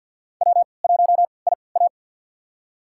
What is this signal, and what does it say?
Morse code, heard as a keyed tone, about 51 wpm, spaced as for 28 wpm, 700 Hz. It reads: W1IA
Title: USA Call Signs - 4 Characters - 3x Character Spacing - 50wpm